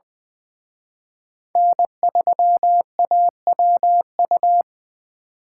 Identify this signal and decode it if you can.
Morse code, heard as a keyed tone, about 24 wpm, 700 Hz. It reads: N3AWU
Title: Speed Racing - USA 1x3 Call Signs 20wpm